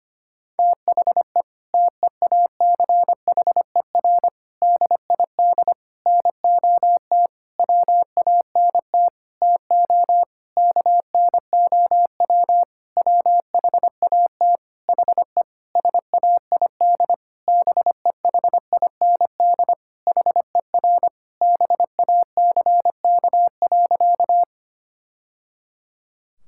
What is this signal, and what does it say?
Morse code, heard as a keyed tone, about 25 wpm, 700 Hz. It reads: THE TEACHER DID NOT WANT TO KNOW WHAT HE SAID BEHIND HER BACK.
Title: Sentences from Top 1000 Words - Review Part 6 - 25wpm